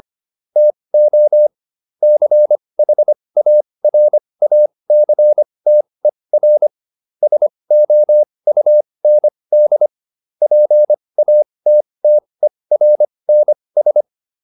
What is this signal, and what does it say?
Morse code, heard as a keyed tone, about 25 wpm, spaced as for 20 wpm, 600 Hz. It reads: TO CHARACTER SOUND PATTERNS